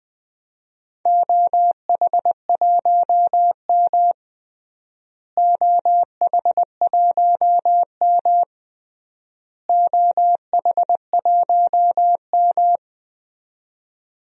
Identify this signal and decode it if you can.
Morse code, heard as a keyed tone, about 20 wpm, 700 Hz. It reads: OH1M OH1M OH1M
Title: DXCC-100 Call Signs - 4 Characters - Repeated 3x - 20wpm